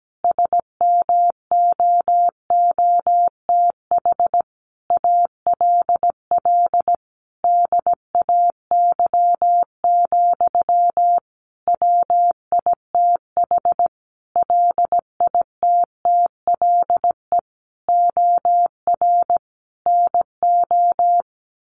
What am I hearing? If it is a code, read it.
Morse code, heard as a keyed tone, about 17 wpm, 700 Hz. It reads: SMOOTH ALL DAY, WITH LITTLE OR NO
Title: Sentences from Top 1400 Words 17wpm